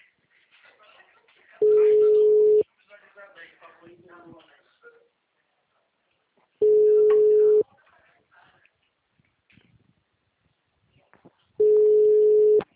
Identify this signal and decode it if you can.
Ringback tone (Europe, 425 Hz), with speech.